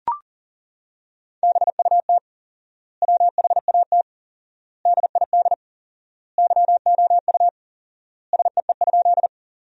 Morse code, heard as a keyed tone, about 40 wpm, 700 Hz. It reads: BUT WHAT DID YOU SEE?
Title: 4x Word-Spacing - Sentences from Top 100 Words - 40wpm